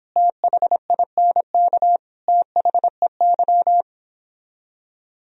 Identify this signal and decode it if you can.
Morse code, heard as a keyed tone, about 29 wpm, 700 Hz. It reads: THINK THEY